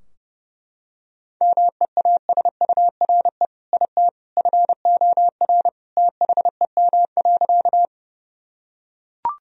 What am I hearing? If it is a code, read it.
Morse code, heard as a keyed tone, about 32 wpm, 700 Hz. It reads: MEASURE IT FOR THEM.
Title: Sentences from Top 1000 Words - Review Part 2 - 30wpm